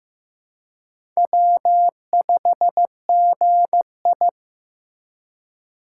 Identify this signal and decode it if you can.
Morse code, heard as a keyed tone, about 15 wpm, 690 Hz. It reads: W5GI
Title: USA Call Signs - 4 Characters - 2x Character Spacing - 15wpm